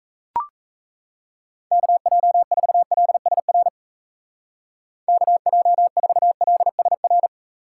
Morse code, heard as a keyed tone, about 34 wpm, 700 Hz. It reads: KJ4LSR KJ4LSR
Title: Speed Racing - USA 2x3 Call Signs 28wpm